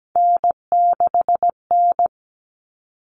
Morse code, heard as a keyed tone, about 17 wpm, 700 Hz. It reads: N6N
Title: All Cut Numbers - 3-Digit 17wpm